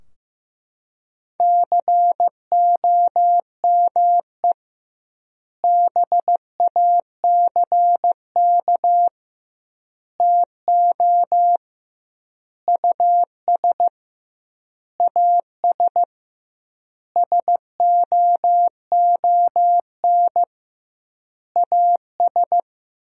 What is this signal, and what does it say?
Morse code, heard as a keyed tone, about 15 wpm, 700 Hz. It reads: COME BACK TO US AS SOON AS
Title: 2x Word-Spacing - Sentences from Top 500 Words - 15wpm